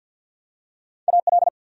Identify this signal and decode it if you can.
Morse code, heard as a keyed tone, about 53 wpm, 700 Hz. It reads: AL
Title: USA Common Names - Max 5 Letters - 50wpm